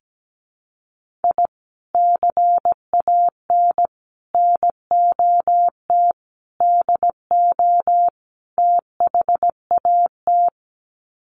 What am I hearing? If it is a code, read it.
Morse code, heard as a keyed tone, about 17 wpm, 700 Hz. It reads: I CAN NOT DO THAT
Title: Sets of 5 Words Encore2 17wpm